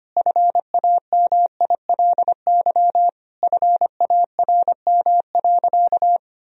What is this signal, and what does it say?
Morse code, heard as a keyed tone, about 25 wpm, 700 Hz. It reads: FAMILY FARM.
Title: Sentences from Top 1000 Words - Review Part 2 - 25wpm